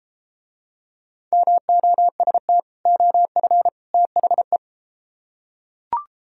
Morse code, heard as a keyed tone, about 33 wpm, 700 Hz. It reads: MOST OF THE